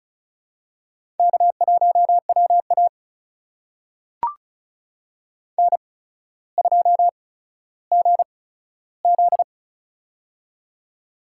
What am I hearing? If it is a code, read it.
Morse code, heard as a keyed tone, about 36 wpm, spaced as for 15 wpm, 700 Hz. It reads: K1WA N2GZ
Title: USA Call Signs - 4 Characters - 8x Character Spacing - 35wpm